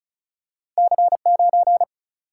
Morse code, heard as a keyed tone, about 36 wpm, 700 Hz. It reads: C9
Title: Single-Letter Single-Number 35wpm